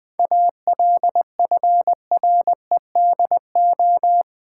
Morse code, heard as a keyed tone, about 20 wpm, 700 Hz. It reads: ALFREDO
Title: Pangram Challenge v2 - 20wpm